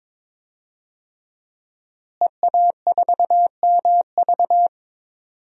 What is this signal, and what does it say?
Morse code, heard as a keyed tone, about 22 wpm, 700 Hz. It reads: EA4MV
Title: DXCC-100 Call Signs - 5 Characters - Repeated 3x - 22wpm